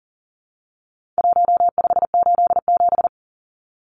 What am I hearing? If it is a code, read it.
Morse code, heard as a keyed tone, about 38 wpm, 700 Hz. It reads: 1587
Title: Numbers - Contest Style - 4-Digit 40wpm